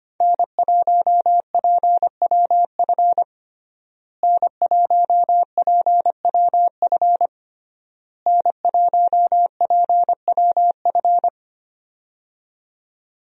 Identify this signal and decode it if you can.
Morse code, heard as a keyed tone, about 25 wpm, 700 Hz. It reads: N1PWF N1PWF N1PWF
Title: USA Call Signs - 5 Characters - Repeated 3x - 25wpm